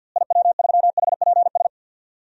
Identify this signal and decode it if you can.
Morse code, heard as a keyed tone, about 53 wpm, 700 Hz. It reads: IW3HPS
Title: DXCC-100 Call Signs - 6 Characters - Repeated 2x - 50wpm